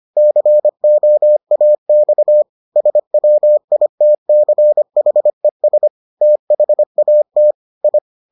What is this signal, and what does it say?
Morse code, heard as a keyed tone, about 25 wpm, 600 Hz. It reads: COAX SWITCHES THAT I